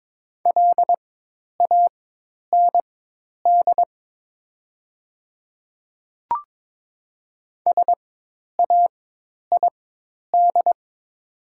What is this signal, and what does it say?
Morse code, heard as a keyed tone, about 22 wpm, spaced as for 10 wpm, 700 Hz. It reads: LAND SAID
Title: 4x Character Spacing - Top 300 Words - 22wpm